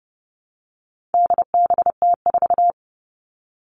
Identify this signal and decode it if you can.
Morse code, heard as a keyed tone, about 30 wpm, 700 Hz. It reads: DBT4
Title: All Cut Numbers - 4-Digit 30wpm